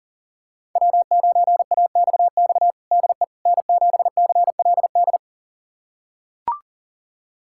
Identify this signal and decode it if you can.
Morse code, heard as a keyed tone, about 40 wpm, 700 Hz. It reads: W9AXX DE N7CLD
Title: USA Callsign DE Callsign - 5 Characters - 40wpm